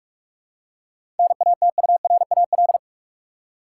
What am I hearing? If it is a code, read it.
Morse code, heard as a keyed tone, about 46 wpm, 700 Hz. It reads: NATURAL
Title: Seven Letter Words - 45wpm